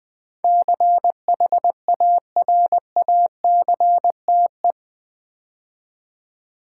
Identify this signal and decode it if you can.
Morse code, heard as a keyed tone, about 20 wpm, 700 Hz. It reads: CHARACTE
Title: Build A Word - 4 Letters At A Time - 12-Letter Words - 20wpm